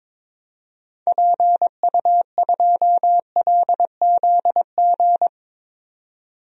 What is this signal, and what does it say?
Morse code, heard as a keyed tone, about 22 wpm, 700 Hz. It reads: PU2LZG
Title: DXCC-100 Call Signs - 6 Characters - 22wpm